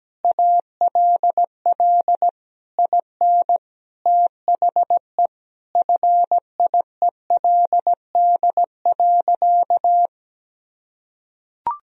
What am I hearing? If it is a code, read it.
Morse code, heard as a keyed tone, about 17 wpm, 700 Hz. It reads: ALL IN THE FIELD.
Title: Sentences from Top 1000 Words - Review Part 2 - 17wpm